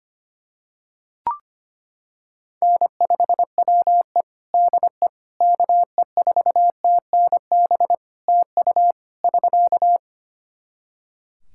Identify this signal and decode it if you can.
Morse code, heard as a keyed tone, about 25 wpm, 700 Hz. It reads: N5WE DE KE4TNB TU <SK>
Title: K1USN SST QSO Practice 25wpm